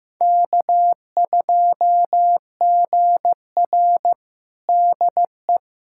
Morse code, heard as a keyed tone, about 15 wpm, 700 Hz. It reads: K2GR DE
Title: USA Callsign DE Callsign - 4 Characters - 15wpm